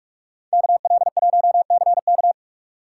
Morse code, heard as a keyed tone, about 54 wpm, 700 Hz. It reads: KL1CK
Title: Speed Racing - USA 2x2 Call Signs 45wpm